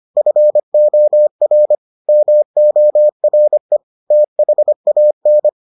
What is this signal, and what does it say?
Morse code, heard as a keyed tone, about 25 wpm, 600 Hz. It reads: FOR MORE THAN